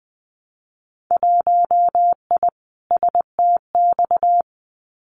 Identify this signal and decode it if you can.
Morse code, heard as a keyed tone, about 20 wpm, 700 Hz. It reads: 1I STX
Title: Winter Field Day Exchanges 20wpm